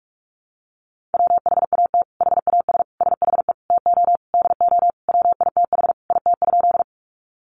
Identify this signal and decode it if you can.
Morse code, heard as a keyed tone, about 44 wpm, 700 Hz. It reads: WHAT HAS SHE TO DO WITH IT?
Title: Sentences from Top 100 Words 45wpm